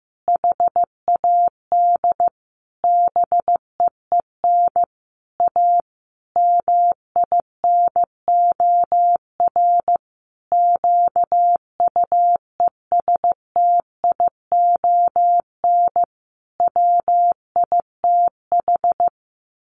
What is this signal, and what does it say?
Morse code, heard as a keyed tone, about 15 wpm, 700 Hz. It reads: HAD BEEN A MINOR QUESTION WITH
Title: Sentences from Top 1500 Words 15wpm